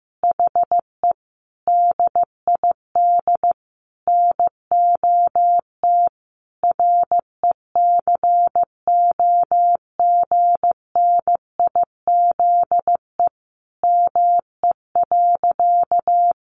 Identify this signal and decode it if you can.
Morse code, heard as a keyed tone, about 15 wpm, 700 Hz. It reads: HE DID NOT RECOGNIZE ME.
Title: Sentences from Top 1400 Words 15wpm